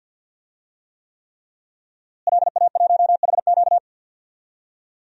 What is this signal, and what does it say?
Morse code, heard as a keyed tone, about 48 wpm, 700 Hz. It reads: LA1HQ
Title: DXCC-100 Call Signs - 5 Characters - Repeated 3x - 50wpm